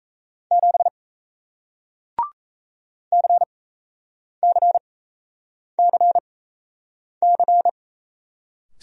Morse code, heard as a keyed tone, about 38 wpm, spaced as for 8 wpm, 700 Hz. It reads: 7 CCCC